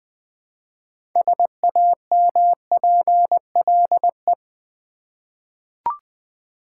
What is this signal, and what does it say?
Morse code, heard as a keyed tone, about 20 wpm, 700 Hz. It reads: SAMPLE
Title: 6 Letter Words - First 3 Letters Given - 20wpm